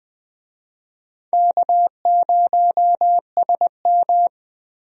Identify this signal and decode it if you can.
Morse code, heard as a keyed tone, about 20 wpm, 700 Hz. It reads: K0SM
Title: USA Call Signs - 4 Characters - Repeated 2x - 20wpm